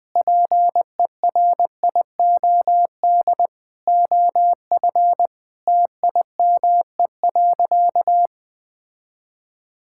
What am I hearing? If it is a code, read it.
Morse code, heard as a keyed tone, about 21 wpm, 700 Hz. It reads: PERIOD OF TIME.
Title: Sentences with 6-Word Phrases - 20wpm